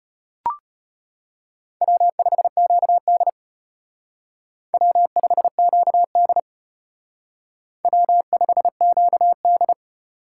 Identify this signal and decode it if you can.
Morse code, heard as a keyed tone, about 31 wpm, 700 Hz. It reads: W5QD W5QD W5QD
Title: Speed Racing - USA 1x2 Call Signs 25wpm